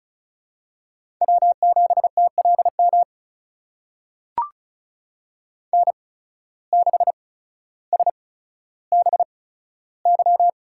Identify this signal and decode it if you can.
Morse code, heard as a keyed tone, about 36 wpm, spaced as for 15 wpm, 700 Hz. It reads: W7TLM N6SBY
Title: USA Call Signs - 8x Character Spacing - 35wpm